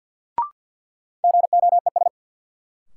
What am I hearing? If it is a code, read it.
Morse code, heard as a keyed tone, about 50 wpm, 700 Hz. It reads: GOES
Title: Sentences from Top 1000 Words - Review Part 5 - New Word Drill - 50wpm